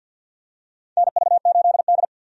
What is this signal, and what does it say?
Morse code, heard as a keyed tone, about 52 wpm, 700 Hz. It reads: NV8D
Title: USA Call Signs - 4 Characters - 50wpm